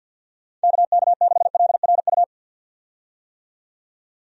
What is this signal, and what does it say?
Morse code, heard as a keyed tone, about 51 wpm, 700 Hz. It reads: KK6LRU
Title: USA Call Signs - 6 Characters - 50wpm